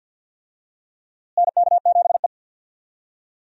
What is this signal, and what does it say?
Morse code, heard as a keyed tone, about 53 wpm, 700 Hz. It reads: NK7E
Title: USA Call Signs - 4 Characters - Repeated 2x - 50wpm